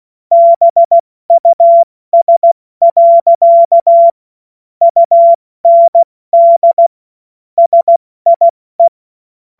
Morse code, heard as a keyed tone, about 16 wpm, spaced as for 14 wpm, 680 Hz. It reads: BUS. UND SIE
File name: Morsekurs / Witze / Witz 05. Oktober 2021, 70 F80